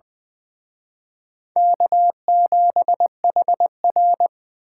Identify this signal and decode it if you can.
Morse code, heard as a keyed tone, about 23 wpm, 700 Hz. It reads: K7HR
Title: Speed Racing - USA 1x2 Call Signs 20wpm